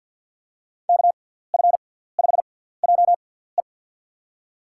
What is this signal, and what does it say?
Morse code, heard as a keyed tone, about 53 wpm, spaced as for 17 wpm, 700 Hz. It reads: KF5JE
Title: USA Call Signs - 6x Character Spacing - 50wpm